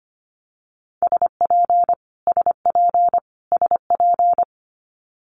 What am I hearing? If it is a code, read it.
Morse code, heard as a keyed tone, about 24 wpm, 700 Hz. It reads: SP SP SP
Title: Multi-letter Phonemes Repeated 3x 25wpm